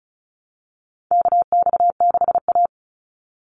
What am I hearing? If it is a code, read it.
Morse code, heard as a keyed tone, about 36 wpm, 700 Hz. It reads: KX6A
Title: NA SOTA Chaser Callsigns 35wpm